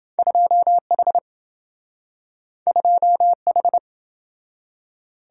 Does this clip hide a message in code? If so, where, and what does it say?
Yes: Morse code from 0.2 to 3.8 s, heard as a keyed tone, about 26 wpm, 700 Hz. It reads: 2H 2H